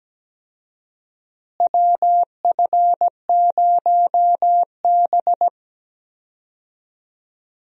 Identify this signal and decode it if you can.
Morse code, heard as a keyed tone, about 17 wpm, 700 Hz. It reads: WF0B
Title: USA Call Signs - 4 Characters - Repeated 3x - 17wpm